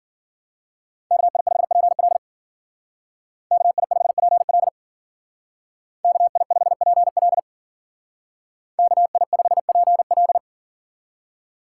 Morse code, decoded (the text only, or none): KI5PL KI5PL KI5PL KI5PL